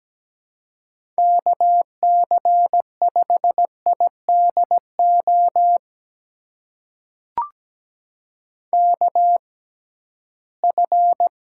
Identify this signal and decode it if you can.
Morse code, heard as a keyed tone, about 17 wpm, spaced as for 13 wpm, 700 Hz. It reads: KC5IDO KF
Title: USA Call Signs - 6 Characters - 6x Character Spacing - 17wpm